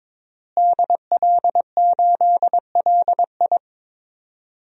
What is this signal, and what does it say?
Morse code, heard as a keyed tone, about 22 wpm, 700 Hz. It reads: DL8LI
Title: DXCC-100 Call Signs - 5 Characters - Repeated 3x - 22wpm